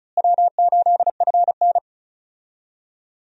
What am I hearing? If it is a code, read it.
Morse code, heard as a keyed tone, about 36 wpm, 700 Hz. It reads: W8FN
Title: USA Call Signs - 4 Characters - Repeated 3x - 35wpm